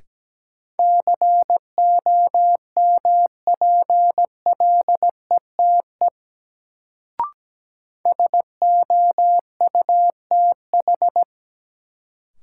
Morse code, heard as a keyed tone, about 17 wpm, 700 Hz. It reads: COMPLETE SOUTH
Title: Sentences from Top 400 Words - New Word Drill - 17wpm